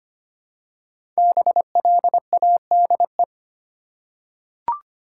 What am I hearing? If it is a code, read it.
Morse code, heard as a keyed tone, about 25 wpm, 700 Hz. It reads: BLADE
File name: 5 Letter Words - First 3 Letters Given - 25wpm